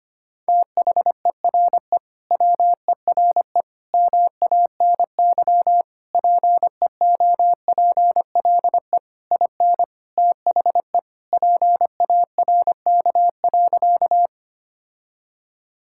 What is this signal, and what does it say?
Morse code, heard as a keyed tone, about 25 wpm, 700 Hz. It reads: THERE WERE MANY PEOPLE IN THE PARK.
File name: practice / 25wpm / Sentences with 3-Word Phrases - 25wpm